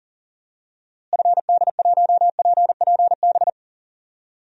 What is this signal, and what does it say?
Morse code, heard as a keyed tone, about 37 wpm, 700 Hz. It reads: FD1PPB
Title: DXCC-100 Call Signs - 6 Characters - Repeated 3x - 40wpm